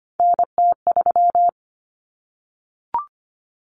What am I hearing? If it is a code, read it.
Morse code, heard as a keyed tone, about 25 wpm, 700 Hz. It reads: NT3
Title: Common Cut Numbers - 3-Digit 25wpm